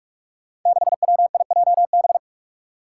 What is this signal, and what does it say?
Morse code, heard as a keyed tone, about 46 wpm, 700 Hz. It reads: 6WIJB
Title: Five Challenging Letter-Number Combinations 45wpm